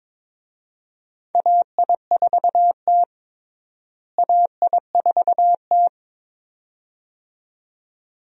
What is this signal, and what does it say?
Morse code, heard as a keyed tone, about 21 wpm, 700 Hz. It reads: AI4T AI4T
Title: USA Call Signs - 4 Characters - Repeated 2x - 22wpm